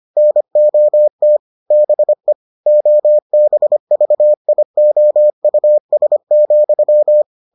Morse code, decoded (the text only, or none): NOT BE OBVIOUS,